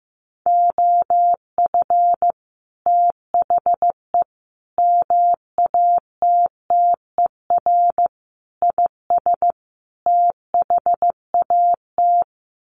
Morse code, heard as a keyed tone, about 15 wpm, 700 Hz. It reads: OF THE MATTER IS THAT
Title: Sets of 5 Words Encore 15wpm